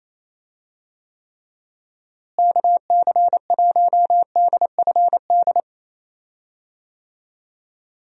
Morse code, heard as a keyed tone, about 28 wpm, 700 Hz. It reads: KC1DFD
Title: USA Call Signs - 6 Characters - Repeated 2x - 28wpm